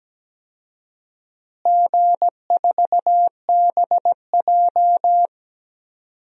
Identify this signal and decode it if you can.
Morse code, heard as a keyed tone, about 17 wpm, 700 Hz. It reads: G4BJ
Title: DXCC-100 Call Signs - 4 Characters - Repeated 2x - 17wpm